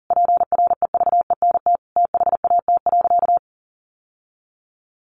Morse code, heard as a keyed tone, about 40 wpm, 700 Hz. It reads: PREVENT THAT.
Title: Sentences from Top 1100 Words 40wpm